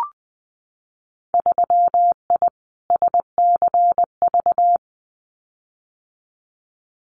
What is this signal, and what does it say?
Morse code, heard as a keyed tone, about 20 wpm, 700 Hz. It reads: 3I SCV
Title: Winter Field Day Exchanges Encore 20wpm